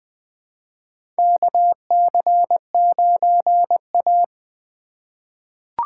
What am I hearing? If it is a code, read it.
Morse code, heard as a keyed tone, about 20 wpm, 700 Hz. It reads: KC9A
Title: USA Call Signs - 4 Characters - 6x Character Spacing - 20wpm